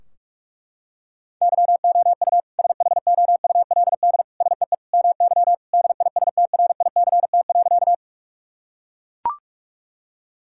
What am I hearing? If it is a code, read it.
Morse code, heard as a keyed tone, about 42 wpm, 700 Hz. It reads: YOU SHOULD SEE MY DISTRICT.
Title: Sentences from Top 1600 Words 45wpm